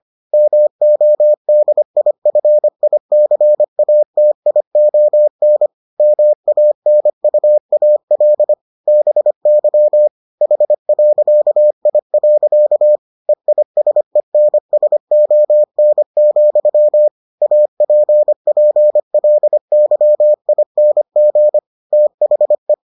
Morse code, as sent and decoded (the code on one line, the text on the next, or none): -- --- -.. .. ..-. .. -.-. .- - .. --- -. / -- .- -. ..- .- .-.. / -... -.-- / .... .-.-.- .. .-.-.- / . .. ... -. ... --- -. --..-- / .- .--. .--. .-.. -.-- .. -. --. / - .... .
MODIFICATION MANUAL BY H.I. EISENSON, APPLYING THE